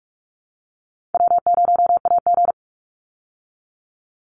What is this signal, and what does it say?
Morse code, heard as a keyed tone, about 44 wpm, 700 Hz. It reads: W0AG
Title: Easy Call Signs Repeated 3x - 45wpm